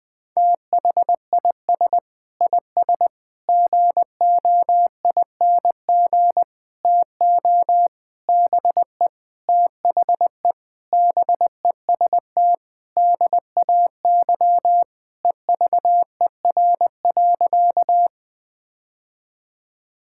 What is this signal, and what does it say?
Morse code, heard as a keyed tone, about 20 wpm, 700 Hz. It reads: THIS IS GOING TO BE THE BEST DAY EVER.
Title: Sentences with 5-Word Phrases - 20wpm